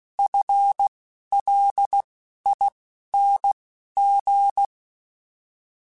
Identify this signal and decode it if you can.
Morse code, heard as a keyed tone, about 16 wpm, spaced as for 12 wpm, 800 Hz. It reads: FLING
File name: FL / FL mp3s / FL Prefixes 3x2 Twice @10